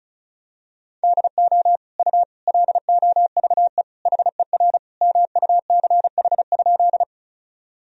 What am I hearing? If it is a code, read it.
Morse code, heard as a keyed tone, about 36 wpm, 700 Hz. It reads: DO U LOVE HER MUCH?
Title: Sentences from Top 400 Words and Abbreviations 35wpm